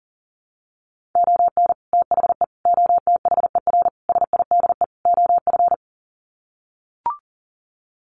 Morse code, heard as a keyed tone, about 37 wpm, 700 Hz. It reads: ON THE OTHER SIDE OF
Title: Sets of 5 Words 40wpm